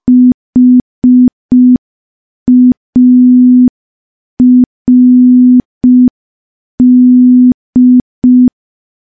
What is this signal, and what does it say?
Morse code, heard as a keyed tone, about 5 wpm, 250 Hz. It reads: HARD
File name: morse / morse-training-05WPM